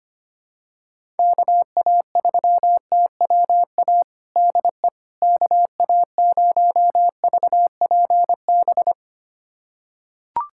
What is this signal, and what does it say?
Morse code, heard as a keyed tone, about 25 wpm, 700 Hz. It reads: KA3TWA DE KA0VPB
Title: USA Callsign DE Callsign - 6 Characters - 25wpm